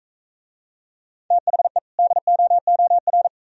Morse code, heard as a keyed tone, about 44 wpm, 700 Hz. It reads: THE DOOR